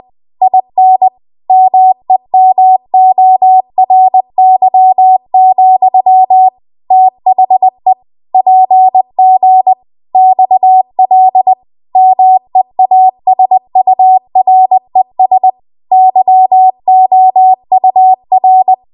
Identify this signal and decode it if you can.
Morse code, heard as a keyed tone, about 20 wpm, 750 Hz. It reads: IN MEMORY, THE PG XL MEASURES YOUR
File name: Archive / 20 WPM / 20WPM